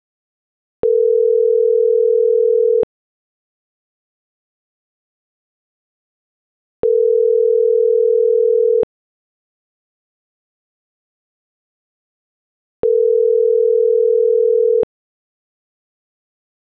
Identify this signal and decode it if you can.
Ringback tone (North America), 440 + 480 Hz.